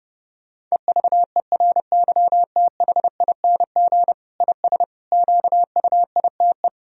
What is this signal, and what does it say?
Morse code, heard as a keyed tone, about 30 wpm, 700 Hz. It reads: EVERYTHING IS QUITE
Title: Sentences from Top 1000 Words - Review Part 2 - 30wpm